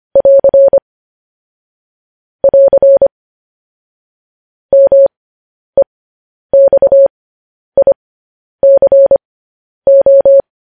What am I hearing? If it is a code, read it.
Morse code, heard as a keyed tone, about 25 wpm, spaced as for 10 wpm, 550 Hz. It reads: <AR> <AR> MEXICO